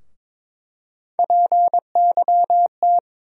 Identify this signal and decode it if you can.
Morse code, heard as a keyed tone, about 22 wpm, 700 Hz. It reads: PYT
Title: USA Call Sign Postfixes - 3 Letters 22wpm